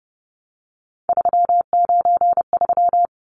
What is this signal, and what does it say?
Morse code, heard as a keyed tone, about 30 wpm, 700 Hz. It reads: 393
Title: Numbers - 3-Digit 30wpm